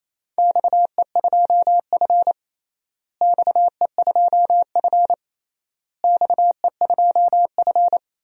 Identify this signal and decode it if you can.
Morse code, heard as a keyed tone, about 28 wpm, 700 Hz. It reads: XE2F XE2F XE2F
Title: DXCC-100 Call Signs - 4 Characters - Repeated 3x - 28wpm